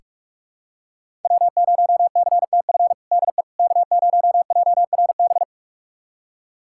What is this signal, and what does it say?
Morse code, heard as a keyed tone, about 41 wpm, 700 Hz. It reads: W0CTF DE K0JRB
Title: USA Callsign DE Callsign - 5 Characters - 45wpm